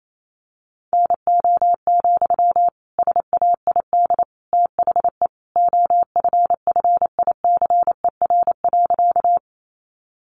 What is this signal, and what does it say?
Morse code, heard as a keyed tone, about 28 wpm, 700 Hz. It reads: NO, SAID THE OFFICER.
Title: Sentences from Top 1200 Words 28wpm